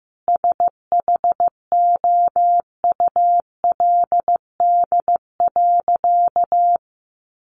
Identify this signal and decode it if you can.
Morse code, heard as a keyed tone, about 15 wpm, 700 Hz. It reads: SHOULD.
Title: Sentences from Top 700 Words 15wpm